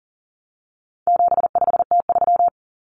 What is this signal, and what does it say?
Morse code, heard as a keyed tone, about 40 wpm, 700 Hz. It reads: 75T3
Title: Common Cut Numbers - 4-Digit 40wpm